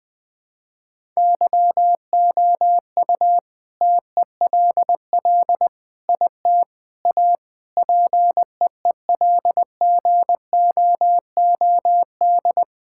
Morse code, heard as a keyed tone, about 21 wpm, 700 Hz. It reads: YOU TELL IT A PEEL GOOD
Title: All-Age Appropriate Jokes - 20wpm